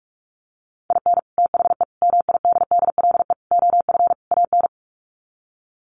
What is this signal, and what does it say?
Morse code, heard as a keyed tone, about 44 wpm, 700 Hz. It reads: IN THE MIDDLE OF AN
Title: Sets of 5 Words Encore2 45wpm